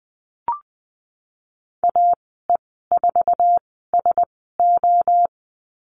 Morse code, heard as a keyed tone, about 20 wpm, spaced as for 15 wpm, 700 Hz. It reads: AE4SO
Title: 088 Lesson 15g - Number 5 - Call Signs 20wpm